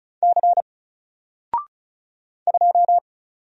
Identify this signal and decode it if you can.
Morse code, heard as a keyed tone, about 34 wpm, 700 Hz. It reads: C 2
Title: Single Letter-Number Alt 35wpm